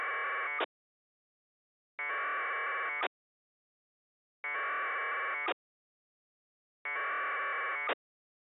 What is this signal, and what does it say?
Packet radio (AX.25, as in APRS).
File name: amsat